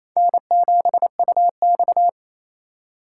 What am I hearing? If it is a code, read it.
Morse code, heard as a keyed tone, about 27 wpm, 700 Hz. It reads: N7UX
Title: USA Call Signs - 2x Character Spacing - 28wpm